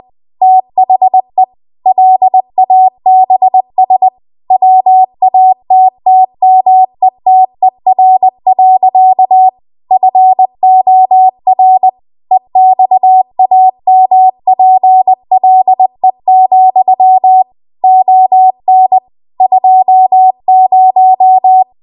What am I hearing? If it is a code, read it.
Morse code, heard as a keyed tone, about 20 wpm, 750 Hz. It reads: THE LABS WATTMETER. FOR EXAMPLE, ON 20
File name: Archive / 20 WPM / 20WPM